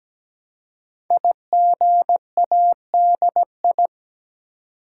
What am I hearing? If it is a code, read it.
Morse code, heard as a keyed tone, about 17 wpm, 700 Hz. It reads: IGADI
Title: Code Groups - 5 Letters - 17wpm